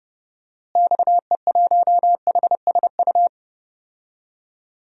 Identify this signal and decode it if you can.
Morse code, heard as a keyed tone, about 30 wpm, 700 Hz. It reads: XE1HSU
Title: DXCC-100 Call Signs - 6 Characters - Repeated 3x - 30wpm